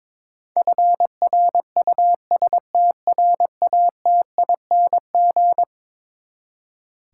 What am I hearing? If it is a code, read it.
Morse code, heard as a keyed tone, about 22 wpm, 700 Hz. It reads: FRUSTRATING
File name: practice / 22wpm / Eleven Letter Words - 22wpm